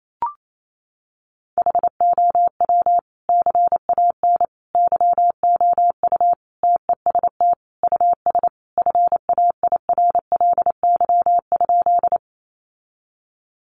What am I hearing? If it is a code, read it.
Morse code, heard as a keyed tone, about 28 wpm, 700 Hz. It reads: HOW CAN YOU TEST US FAIRLY?
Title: Sentences from Top 1100 Words 28wpm